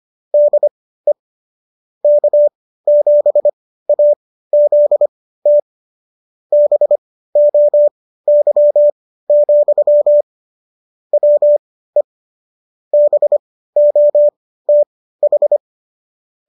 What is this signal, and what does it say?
Morse code, heard as a keyed tone, about 25 wpm, spaced as for 15 wpm, 590 Hz. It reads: DE K7AZT BOY, WE BOTH